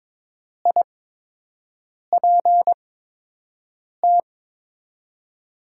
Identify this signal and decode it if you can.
Morse code, heard as a keyed tone, about 22 wpm, spaced as for 6 wpm, 700 Hz. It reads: IPT
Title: USA Call Signs - 8x Character Spacing - 22wpm